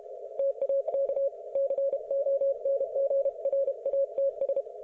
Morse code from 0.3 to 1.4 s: TAK